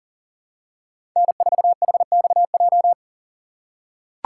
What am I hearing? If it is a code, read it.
Morse code, heard as a keyed tone, about 40 wpm, 700 Hz. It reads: N4HXJ
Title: USA Call Signs - 5 Characters - 6x Character Spacing - 40wpm